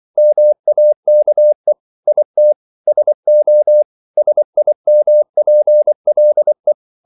Morse code, heard as a keyed tone, about 24 wpm, 600 Hz. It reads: MAKE IT SO SIMPLE
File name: CW / Practice / EnglishQuotations0224